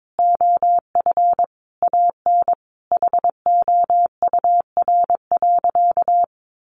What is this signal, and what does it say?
Morse code, heard as a keyed tone, about 22 wpm, 700 Hz. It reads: OF AN HOUR.
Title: Sentences from Top 1500 Words 22wpm